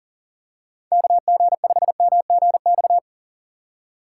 Morse code, heard as a keyed tone, about 40 wpm, 700 Hz. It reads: KG5MGX
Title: USA Call Signs - 6 Characters - 4x Character Spacing - 40wpm